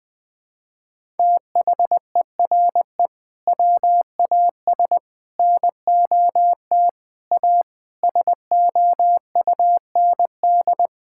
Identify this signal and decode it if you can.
Morse code, heard as a keyed tone, about 20 wpm, 700 Hz. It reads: THERE WAS NOT A SOUND
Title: Sentences from Top 1000 Words - Review Part 6 - 20wpm